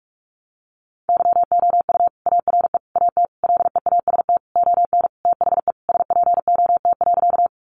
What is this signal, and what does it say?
Morse code, heard as a keyed tone, about 44 wpm, 700 Hz. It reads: YOU ARE AT LEAST ON THE SPOT.
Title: Sentences from Top 1100 Words 45wpm